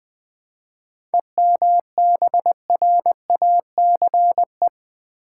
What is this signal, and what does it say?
Morse code, heard as a keyed tone, about 20 wpm, 690 Hz. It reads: EMBRACE
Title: 7 Letter Words - First 3 Letters Given - 20wpm